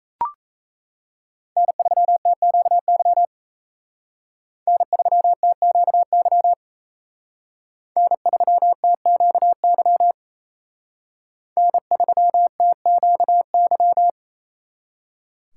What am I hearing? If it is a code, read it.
Morse code, heard as a keyed tone, about 32 wpm, 700 Hz. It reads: N3TQY N3TQY N3TQY N3TQY
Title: Speed Racing - USA 1x3 Call Signs 28wpm